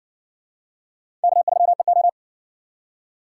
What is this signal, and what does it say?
Morse code, heard as a keyed tone, about 55 wpm, 700 Hz. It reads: K3EO